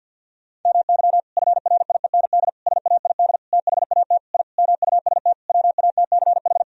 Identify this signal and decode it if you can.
Morse code, heard as a keyed tone, about 50 wpm, 700 Hz. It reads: MY FRIEND SAID THAT I MUST WATCH
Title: Sentences from Top 1000 Words - Review Part 4 - 50wpm